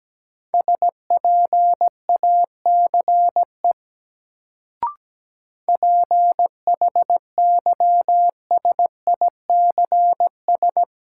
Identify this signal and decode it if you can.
Morse code, heard as a keyed tone, about 17 wpm, 700 Hz. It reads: SPACE PHYSICS
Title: Sentences from Top 1000 Words - Review Part 5 - New Word Drill - 17wpm